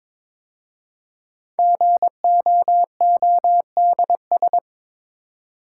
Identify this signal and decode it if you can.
Morse code, heard as a keyed tone, about 22 wpm, 700 Hz. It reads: GOODS
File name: 5 Letter Words - First 3 Letters Given - 22wpm